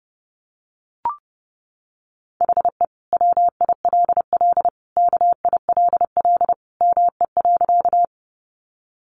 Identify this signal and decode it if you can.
Morse code, heard as a keyed tone, about 28 wpm, 700 Hz. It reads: HE WILL KILL ME.
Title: Sentences from Top 1400 Words 30wpm